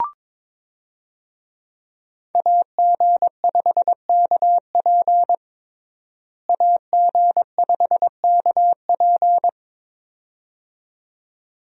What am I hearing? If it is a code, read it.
Morse code, heard as a keyed tone, about 22 wpm, 700 Hz. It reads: AG5KP AG5KP
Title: USA Call Signs - 5 Characters - Repeated 2x - 22wpm